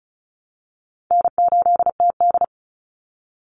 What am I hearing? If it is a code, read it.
Morse code, heard as a keyed tone, about 34 wpm, 700 Hz. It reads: N8TD
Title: Call Signs Repeated 3x - 35wpm